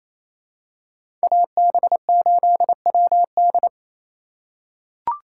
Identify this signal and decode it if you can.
Morse code, heard as a keyed tone, about 28 wpm, 700 Hz. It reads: AB8WD
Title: USA Call Signs - 5 Characters - 2x Character Spacing - 28wpm